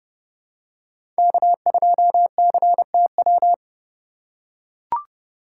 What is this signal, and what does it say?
Morse code, heard as a keyed tone, about 30 wpm, 700 Hz. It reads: K2CTW